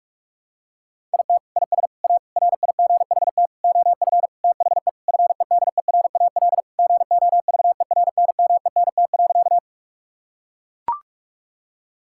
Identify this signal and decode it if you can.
Morse code, heard as a keyed tone, about 41 wpm, 700 Hz. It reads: IT IS A RIGHT OF THE FEDERAL GOVERNMENT.
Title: Sentences from Top 1000 Words - Review Part 5 - 45wpm